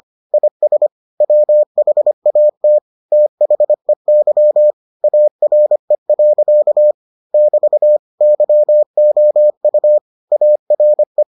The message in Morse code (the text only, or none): IS WHAT THEY ARE. <BT> YOU ARE